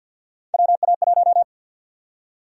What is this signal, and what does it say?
Morse code, heard as a keyed tone, about 51 wpm, 700 Hz. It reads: WA1
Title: USA Call Sign Prefixes - 50wpm